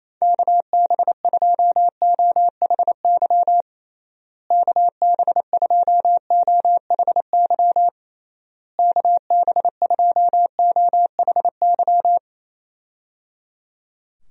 Morse code, heard as a keyed tone, about 28 wpm, 700 Hz. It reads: KB2OHY KB2OHY KB2OHY
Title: USA Call Signs - 6 Characters - Repeated 3x - 28wpm